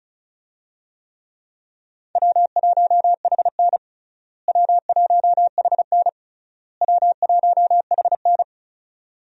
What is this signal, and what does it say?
Morse code, heard as a keyed tone, about 34 wpm, 700 Hz. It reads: W1HN W1HN W1HN
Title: USA Call Signs - 4 Characters - Repeated 3x - 35wpm